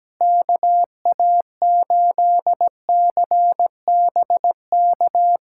Morse code, heard as a keyed tone, about 17 wpm, 700 Hz. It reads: KA8CBK
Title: USA Callsign DE Callsign - 6 Characters - 17wpm